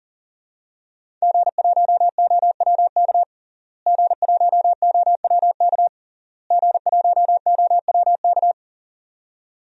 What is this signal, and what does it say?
Morse code, heard as a keyed tone, about 37 wpm, 700 Hz. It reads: G1OWK G1OWK G1OWK
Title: DXCC-100 Call Signs - 5 Characters - Repeated 3x - 40wpm